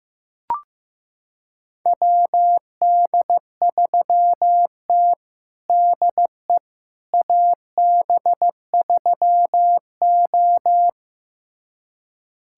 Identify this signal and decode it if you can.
Morse code, heard as a keyed tone, about 15 wpm, 700 Hz. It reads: WD3T DE AB3O
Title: USA Callsign DE Callsign - 4 Characters - 15wpm